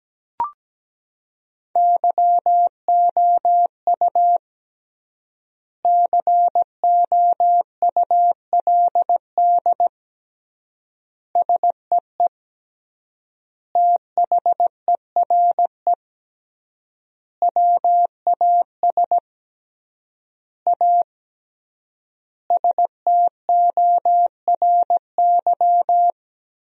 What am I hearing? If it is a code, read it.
Morse code, heard as a keyed tone, about 17 wpm, 700 Hz. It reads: YOU COULD SEE THERE WAS A STORY
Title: 3x Word-Spacing - Sentences from Top 500 Words - 17wpm